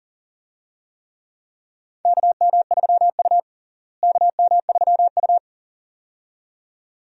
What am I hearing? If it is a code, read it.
Morse code, heard as a keyed tone, about 40 wpm, 700 Hz. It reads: KM3U KM3U
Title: USA Call Signs - 4 Characters - Repeated 2x - 40wpm